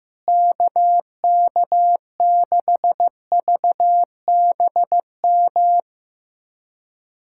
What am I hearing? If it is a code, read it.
Morse code, heard as a keyed tone, about 15 wpm, 700 Hz. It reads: KK6VBM
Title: USA Callsign DE Callsign - 6 Characters - 15wpm